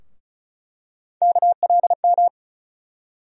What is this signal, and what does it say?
Morse code, heard as a keyed tone, about 34 wpm, 700 Hz. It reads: KLM
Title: USA Call Sign Postfixes - 3 Letters 35wpm